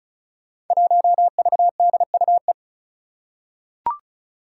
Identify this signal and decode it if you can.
Morse code, heard as a keyed tone, about 36 wpm, 700 Hz. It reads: 1VDUE
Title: Five Challenging Letter-Number Combinations 35wpm